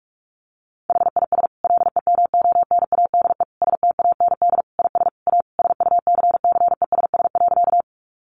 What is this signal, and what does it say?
Morse code, heard as a keyed tone, about 44 wpm, 700 Hz. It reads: HIS LEMONADE STAND IS A SUCCESS.
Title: Additional Easy Phrases 45wpm